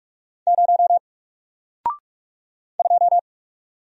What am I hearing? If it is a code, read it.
Morse code, heard as a keyed tone, about 44 wpm, 700 Hz. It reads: 0 2
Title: Single Character (Letter:Number:Punctuation:Prosign) 45wpm